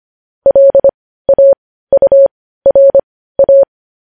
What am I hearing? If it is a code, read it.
Morse code, heard as a keyed tone, about 25 wpm, spaced as for 15 wpm, 550 Hz. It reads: LAURA
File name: podcast / News headlines-1x-25x15